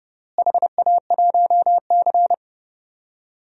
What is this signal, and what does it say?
Morse code, heard as a keyed tone, about 32 wpm, 700 Hz. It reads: HA1C
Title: DXCC-100 Call Signs - 4 Characters - Repeated 3x - 30wpm